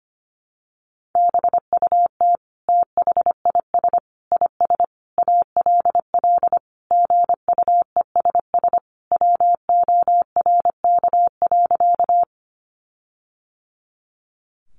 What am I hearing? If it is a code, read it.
Morse code, heard as a keyed tone, about 25 wpm, 700 Hz. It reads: BUT THIS IS ALL GUESS WORK.
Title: Sentences from Top 1500 Words 25wpm